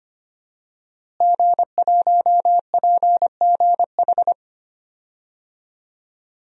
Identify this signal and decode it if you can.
Morse code, heard as a keyed tone, about 25 wpm, 700 Hz. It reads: G1PGH
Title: DXCC-100 Call Signs - 5 Characters - Repeated 2x - 25wpm